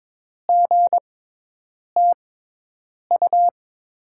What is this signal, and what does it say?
Morse code, heard as a keyed tone, about 22 wpm, spaced as for 8 wpm, 700 Hz. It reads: GTU
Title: USA Call Signs - 6 Characters - 6x Character Spacing - 22wpm